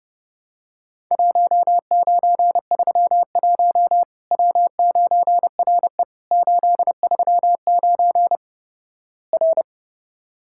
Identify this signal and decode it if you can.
Morse code, heard as a keyed tone, about 32 wpm, 700 Hz. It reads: 1931 W9RE 839 R
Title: Worked All Europe DX Contest - QTC Exchange - 30wpm